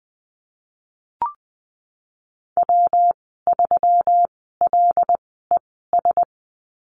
Morse code, heard as a keyed tone, about 20 wpm, spaced as for 15 wpm, 700 Hz. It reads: W3LES
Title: 128 Lesson 21g - Number 6 - Call Signs 20wpm